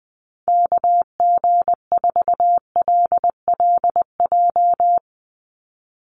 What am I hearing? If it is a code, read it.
Morse code, heard as a keyed tone, about 20 wpm, 700 Hz. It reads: KG4LLJ
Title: USA Call Signs 20wpm